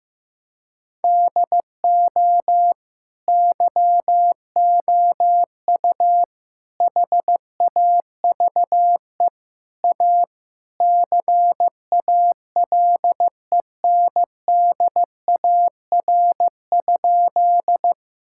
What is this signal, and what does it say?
Morse code, heard as a keyed tone, about 15 wpm, 700 Hz. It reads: DO YOU HAVE A CALENDAR?
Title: Sentences from Top 1600 Words 15wpm